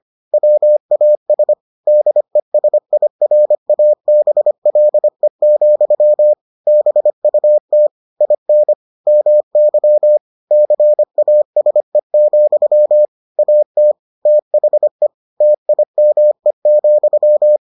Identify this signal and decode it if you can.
Morse code, heard as a keyed tone, about 25 wpm, 600 Hz. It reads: WAS DESIRABLE, BUT IN MY CASE, AT THE TIME,